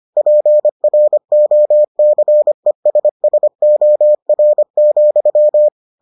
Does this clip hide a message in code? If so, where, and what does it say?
Yes: Morse code from 0.2 to 5.7 s, heard as a keyed tone, about 25 wpm, 600 Hz. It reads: PROCESSOR,